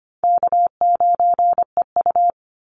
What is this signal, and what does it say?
Morse code, heard as a keyed tone, about 25 wpm, 700 Hz. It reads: K9EU
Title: Easy Call Signs Repeated 2x - 25wpm